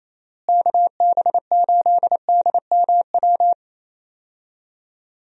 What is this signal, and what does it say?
Morse code, heard as a keyed tone, about 28 wpm, 700 Hz. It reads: KB8DMW
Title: USA Call Signs - 6 Characters - Repeated 2x - 28wpm